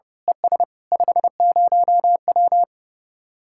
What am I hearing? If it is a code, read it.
Morse code, heard as a keyed tone, about 28 wpm, 700 Hz. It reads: ES 50W
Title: Speed-Racing Common QSO Phrases - 30wpm